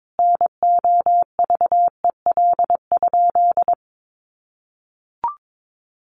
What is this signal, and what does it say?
Morse code, heard as a keyed tone, about 22 wpm, 700 Hz. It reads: NOVEL?
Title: Sentences from Top 1300 Words 22wpm